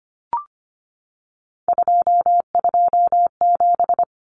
Morse code, heard as a keyed tone, about 25 wpm, 700 Hz. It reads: 227
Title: Common Cut Numbers - 3-Digit 25wpm